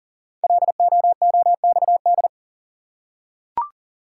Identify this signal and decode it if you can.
Morse code, heard as a keyed tone, about 38 wpm, 690 Hz. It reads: LOOXD
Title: Code Groups - 5 Letter:Numbers - 40wpm